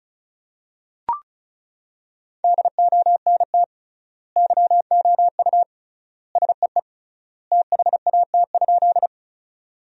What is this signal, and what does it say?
Morse code, heard as a keyed tone, about 36 wpm, 700 Hz. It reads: DONT YOU SEE THAT?
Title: 3x Word-Spacing - Sentences from Top 300 Words - 35wpm